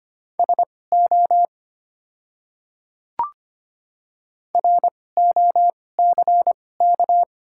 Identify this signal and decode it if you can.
Morse code, heard as a keyed tone, about 25 wpm, spaced as for 18 wpm, 700 Hz. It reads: SO ROCK
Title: 2x Character Spacing - Top 500 Words - 25wpm